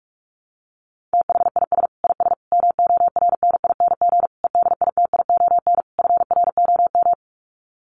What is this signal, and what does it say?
Morse code, heard as a keyed tone, about 44 wpm, 700 Hz. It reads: THIS IS MORNING EDITION FROM
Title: Sets of 5 Words Encore2 45wpm